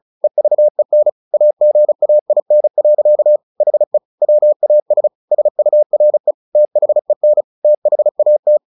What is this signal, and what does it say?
Morse code, heard as a keyed tone, about 35 wpm, 600 Hz. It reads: EVEN AGAIN. HE WAS SURE THEN THAT
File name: the old man and the sea 2 35wpm